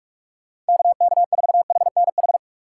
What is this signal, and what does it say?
Morse code, heard as a keyed tone, about 59 wpm, 700 Hz. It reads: KK4HNH